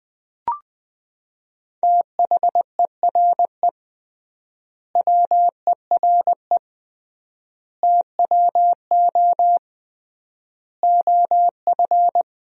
Morse code, heard as a keyed tone, about 21 wpm, 700 Hz. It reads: THERE WERE TWO OF
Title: 3x Word-Spacing - Sentences from Top 100 Words - 20wpm